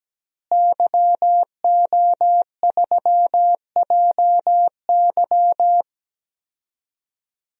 Morse code, heard as a keyed tone, about 17 wpm, 700 Hz. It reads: YO3JY
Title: DXCC-100 Call Signs - 5 Characters - Repeated 3x - 17wpm